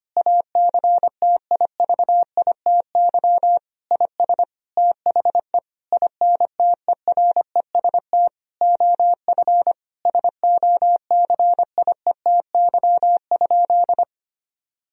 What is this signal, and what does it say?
Morse code, heard as a keyed tone, about 25 wpm, 700 Hz. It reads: ACTIVITY IS THE INTEREST OF SOCIETY?
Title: Sentences from Top 1000 Words - Review Part 4 - 25wpm